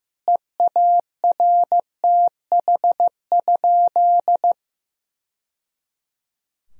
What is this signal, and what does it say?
Morse code, heard as a keyed tone, about 15 wpm, 700 Hz. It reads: EARTH?
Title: Sentences from Top 1000 Words - Review Part 3 - 15wpm